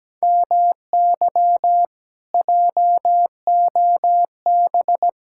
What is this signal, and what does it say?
Morse code, heard as a keyed tone, about 17 wpm, 700 Hz. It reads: MY JOB
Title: Speed-Racing Common QSO Phrases - 17wpm